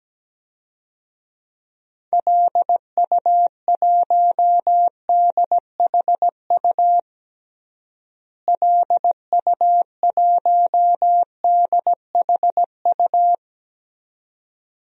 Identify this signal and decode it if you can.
Morse code, heard as a keyed tone, about 17 wpm, 700 Hz. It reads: LU1DHU LU1DHU
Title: DXCC-100 Call Signs - 6 Characters - Repeated 2x - 17wpm